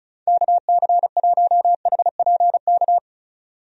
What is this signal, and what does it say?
Morse code, heard as a keyed tone, about 34 wpm, 700 Hz. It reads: KC1HPK
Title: USA Call Signs - 6 Characters - 35wpm